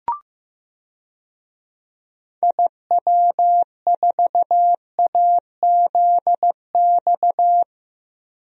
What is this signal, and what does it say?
Morse code, heard as a keyed tone, about 15 wpm, 700 Hz. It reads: IW4AZX
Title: DXCC-100 Call Signs - 6 Characters - 15wpm